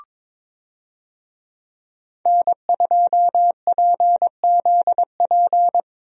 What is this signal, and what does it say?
Morse code, heard as a keyed tone, about 22 wpm, 700 Hz. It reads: N2PZP
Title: USA Call Signs - 5 Characters - Repeated 2x - 22wpm